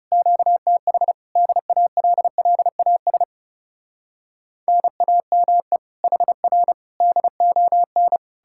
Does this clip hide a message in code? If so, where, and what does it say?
Yes: Morse code from 0.1 to 8.2 s, heard as a keyed tone, about 40 wpm, 700 Hz. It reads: QTH DALLAS NAME HR DON